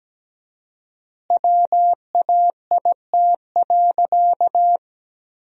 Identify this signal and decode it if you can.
Morse code, heard as a keyed tone, about 17 wpm, 700 Hz. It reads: WAIT.
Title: 4x Word-Spacing - Sentences from Top 500 Words - 17wpm